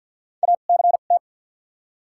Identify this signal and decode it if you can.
Morse code, heard as a keyed tone, about 53 wpm, 700 Hz. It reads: A C T
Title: 2x Character Spacing - Top 500 Words - 50wpm